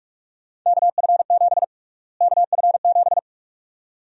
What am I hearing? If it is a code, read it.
Morse code, heard as a keyed tone, about 44 wpm, 700 Hz. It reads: KF7 KF7
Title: USA Call Sign Prefixes - Repeated 2x - 45wpm